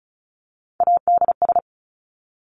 Morse code, heard as a keyed tone, about 36 wpm, 700 Hz. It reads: ADS
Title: Abbreviations 35wpm